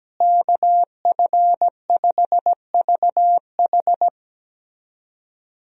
Morse code, heard as a keyed tone, about 17 wpm, 700 Hz. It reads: KF5VH
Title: USA Call Signs - 5 Characters - Repeated 3x - 17wpm